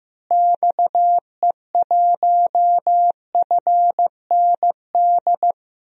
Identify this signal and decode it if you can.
Morse code, heard as a keyed tone, about 15 wpm, 700 Hz. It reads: XE1FND
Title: DXCC-100 Call Signs - 6 Characters - Repeated 3x - 15wpm